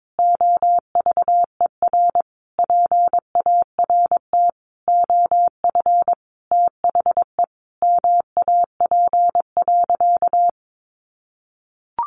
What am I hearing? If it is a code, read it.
Morse code, heard as a keyed tone, about 22 wpm, 700 Hz. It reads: OVER PART OF THE MAP.
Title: Sentences from Top 400 Words 22wpm